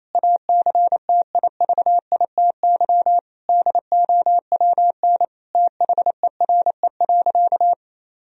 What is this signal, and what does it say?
Morse code, heard as a keyed tone, about 28 wpm, 700 Hz. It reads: ACTIVITY DOWN THERE.
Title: Sentences from Top 1000 Words - Review Part 4 - 28wpm